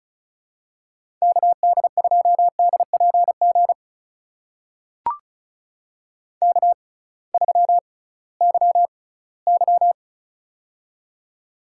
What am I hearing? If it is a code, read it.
Morse code, heard as a keyed tone, about 36 wpm, spaced as for 21 wpm, 700 Hz. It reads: KD2DPG K3YY
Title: USA Call Signs - 6x Character Spacing - 35wpm